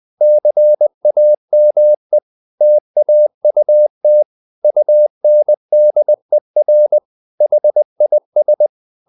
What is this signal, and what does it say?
Morse code, heard as a keyed tone, about 20 wpm, 600 Hz. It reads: CAME TAUT UNDER HIS
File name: the old man and the sea / the old man and the sea part 1 20wpm